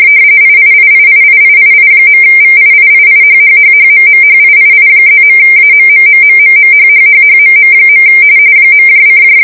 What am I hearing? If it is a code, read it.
RTTY, about 45.45 baud. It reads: IIIIIIIII.VIIIIHHHHMMHIII+.AMMMMMMHHHHMM. .: